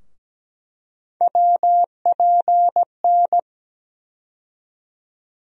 Morse code, heard as a keyed tone, about 17 wpm, 700 Hz. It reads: WPN
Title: USA Call Sign Postfixes - 3 Letters 17wpm